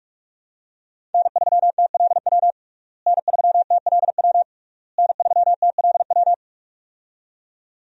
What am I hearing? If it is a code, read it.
Morse code, heard as a keyed tone, about 43 wpm, 700 Hz. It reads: N3TLW N3TLW N3TLW
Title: USA Call Signs - 5 Characters - Repeated 3x - 45wpm